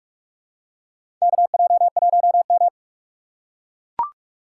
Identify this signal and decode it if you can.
Morse code, heard as a keyed tone, about 44 wpm, 700 Hz. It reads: KJ1M
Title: USA Call Signs - 4 Characters - 2x Character Spacing - 45wpm